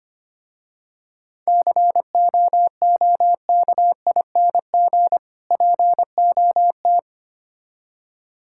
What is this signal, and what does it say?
Morse code, heard as a keyed tone, about 26 wpm, 700 Hz. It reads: COOKING POT
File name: practice / 25wpm / Difficult Words - Speed-Racing with Hints - 25wpm